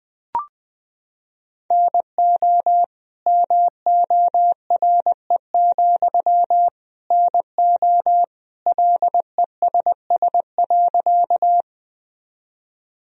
Morse code, heard as a keyed tone, about 20 wpm, 700 Hz. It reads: NO MORE, NO LESS.
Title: Sentences from Top 1000 Words - Review Part 2 - 20wpm